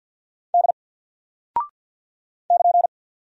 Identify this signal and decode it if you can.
Morse code, heard as a keyed tone, about 48 wpm, 700 Hz. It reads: D <KN>
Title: Single Character (Letter:Number:Punctuation:Prosign) 50wpm